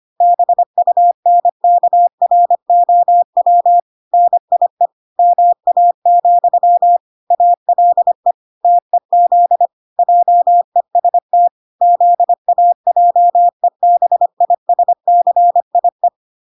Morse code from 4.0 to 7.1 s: NIE MA,